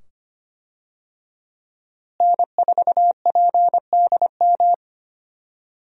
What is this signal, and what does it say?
Morse code, heard as a keyed tone, about 25 wpm, 700 Hz. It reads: N4PDM
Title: USA Call Signs - 5 Characters - Repeated 2x - 25wpm